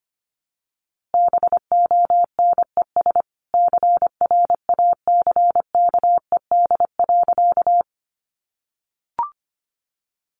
Morse code, heard as a keyed tone, about 25 wpm, 700 Hz. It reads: BONES CRACKED.
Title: A Sound Of Thunder 25wpm